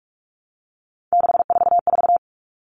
Morse code, heard as a keyed tone, about 44 wpm, 700 Hz. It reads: 644
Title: Numbers - 3-Digit 45wpm